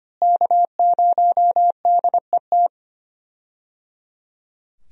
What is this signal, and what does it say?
Morse code, heard as a keyed tone, about 25 wpm, 700 Hz. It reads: K0DET
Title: USA Call Signs - 5 Characters - Repeated 3x - 25wpm